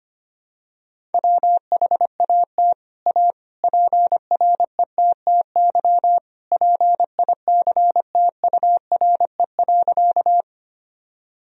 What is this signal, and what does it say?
Morse code, heard as a keyed tone, about 25 wpm, 700 Hz. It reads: WHAT A PRETTY PICTURE.
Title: Sentences from Top 1000 Words - Review Part 1 - 25wpm